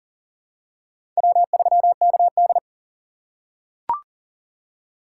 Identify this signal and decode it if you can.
Morse code, heard as a keyed tone, about 37 wpm, 690 Hz. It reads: W3KD
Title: USA Call Signs - 4 Characters - 2x Character Spacing - 40wpm